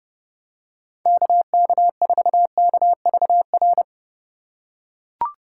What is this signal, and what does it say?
Morse code, heard as a keyed tone, about 30 wpm, 700 Hz. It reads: KK4KVR